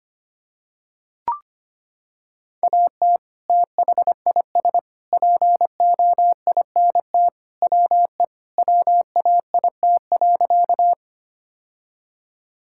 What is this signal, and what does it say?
Morse code, heard as a keyed tone, about 25 wpm, 700 Hz. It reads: AT THIS POINT WE WAIT.
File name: practice / 25wpm / Sentences with 3-Word Phrases - 25wpm